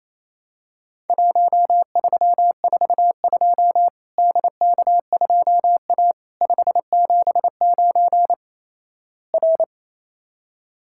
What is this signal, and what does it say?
Morse code, heard as a keyed tone, about 28 wpm, 700 Hz. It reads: 1342 DK2A 579 R
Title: Worked All Europe DX Contest - QTC Exchange - 28wpm